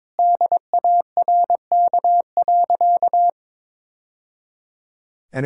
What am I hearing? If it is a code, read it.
Morse code, heard as a keyed tone, about 22 wpm, 700 Hz. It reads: DARK.
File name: practice / 22wpm / Sentences with 3-Word Phrases - 22wpm